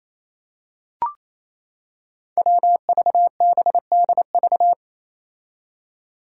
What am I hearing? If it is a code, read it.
Morse code, heard as a keyed tone, about 28 wpm, 700 Hz. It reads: WVBDV
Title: Five Challenging Letter-Number Combinations 28wpm